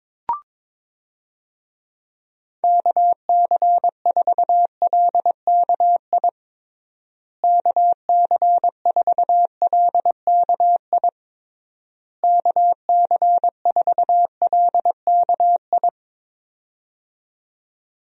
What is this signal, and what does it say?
Morse code, heard as a keyed tone, about 22 wpm, 700 Hz. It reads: KC4LKI KC4LKI KC4LKI
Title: USA Call Signs - 6 Characters - Repeated 3x - 22wpm